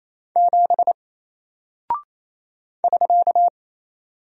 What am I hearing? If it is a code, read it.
Morse code, heard as a keyed tone, about 28 wpm, 700 Hz. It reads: 7 <SK>
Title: Single Character (Letter:Number:Punctuation:Prosign) Alt 28wpm